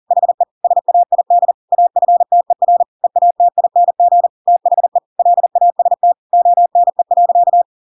Morse code, heard as a keyed tone, about 40 wpm, 700 Hz. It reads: HE SAID AFTER EATING THE LAST ONE.
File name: Pinocchio / Pinocchio ch7 40 40 ik1hge